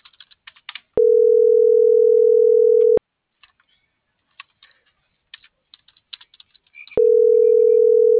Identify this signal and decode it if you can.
Ringback tone (North America), 440 + 480 Hz.